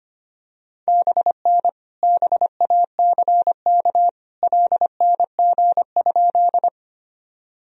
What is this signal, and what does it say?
Morse code, heard as a keyed tone, about 25 wpm, 700 Hz. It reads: BN BACK LNG?
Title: Sentences from Top 200 Words and Abbreviations 25wpm